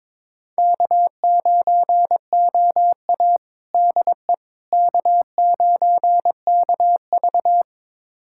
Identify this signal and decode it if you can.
Morse code, heard as a keyed tone, about 22 wpm, 700 Hz. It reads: K9OA DE K9KV
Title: USA Callsign DE Callsign - 4 Characters - 22wpm